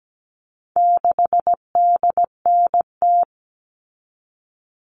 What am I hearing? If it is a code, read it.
Morse code, heard as a keyed tone, about 17 wpm, 700 Hz. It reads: 6DNT
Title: All Cut Numbers - 4-Digit 17wpm